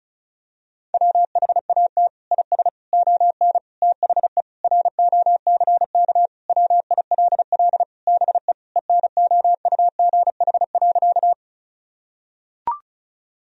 Morse code, heard as a keyed tone, about 36 wpm, 700 Hz. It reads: WHAT IS ON THE ROCK WILL BE ENOUGH.
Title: Sentences from Top 1000 Words - Review Part 2 - 35wpm